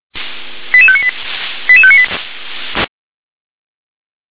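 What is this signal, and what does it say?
Five-tone selective call (ZVEI), heard as quick runs of five tones, tones of about 70 ms. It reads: 88598 88598
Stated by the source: Five-tone ZVEI selective call received on SDR.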